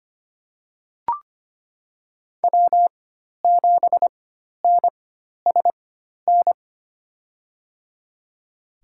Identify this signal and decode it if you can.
Morse code, heard as a keyed tone, about 25 wpm, spaced as for 12 wpm, 700 Hz. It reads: W7NSN
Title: USA Call Signs - 4x Character Spacing - 25wpm